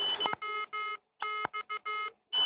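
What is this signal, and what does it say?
Morse code, heard as a keyed tone, about 15 wpm, 1280 Hz. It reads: WX
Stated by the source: Morse code from GI0RQK, worked on 80 m with low power from the Rozkmit transceiver.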